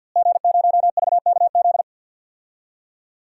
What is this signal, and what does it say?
Morse code, heard as a keyed tone, about 51 wpm, 700 Hz. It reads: G0VKZ
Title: DXCC-100 Call Signs - 5 Characters - Repeated 2x - 50wpm